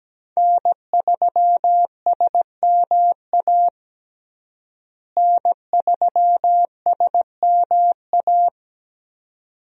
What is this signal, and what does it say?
Morse code, heard as a keyed tone, about 17 wpm, 700 Hz. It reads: N3SMA N3SMA
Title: USA Call Signs - 5 Characters - Repeated 2x - 17wpm